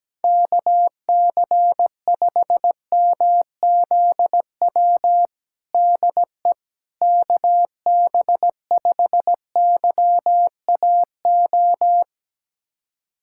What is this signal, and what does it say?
Morse code, heard as a keyed tone, about 17 wpm, 700 Hz. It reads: KC5MZW DE KB5YAO
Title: USA Callsign DE Callsign - 6 Characters - 17wpm